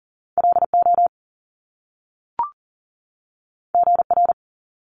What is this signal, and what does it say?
Morse code, heard as a keyed tone, about 38 wpm, 700 Hz. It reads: LO GR
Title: Top Two Letter Combinations in Words 40wpm